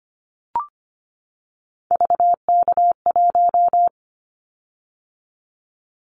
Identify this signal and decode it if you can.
Morse code, heard as a keyed tone, about 25 wpm, 700 Hz. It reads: VK1
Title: Callsign Prefixes for Top 50 DXCC - 25wpm